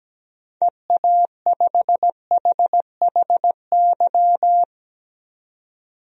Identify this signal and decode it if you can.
Morse code, heard as a keyed tone, about 17 wpm, 700 Hz. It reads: EA5HHY
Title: DXCC-100 Call Signs - 6 Characters - 17wpm